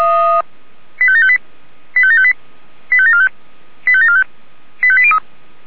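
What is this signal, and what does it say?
Five-tone selective call (ZVEI), heard as quick runs of five tones, tones of about 80 ms. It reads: 85758 85758 85745 85745 85803
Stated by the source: ZVEI five-tone selective call received by SDR.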